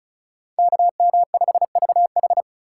Morse code, heard as a keyed tone, about 36 wpm, 700 Hz. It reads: KM5VH
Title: USA Call Signs - 5 Characters - 2x Character Spacing - 35wpm